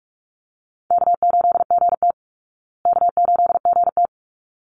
Morse code, heard as a keyed tone, about 44 wpm, 700 Hz. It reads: K8GT K8GT
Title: Easy Call Signs Repeated 2x - 45wpm